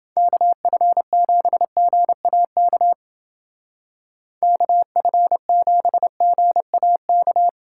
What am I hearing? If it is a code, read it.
Morse code, heard as a keyed tone, about 29 wpm, 700 Hz. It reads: KF7GAK KF7GAK